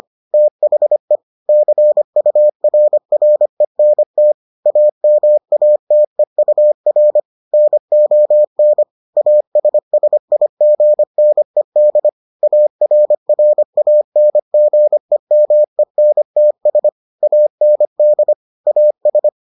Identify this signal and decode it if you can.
Morse code, heard as a keyed tone, about 25 wpm, 600 Hz. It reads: THE CURRENT AMATEUR NON ASSIGNED ARRANGEMENTS AND AS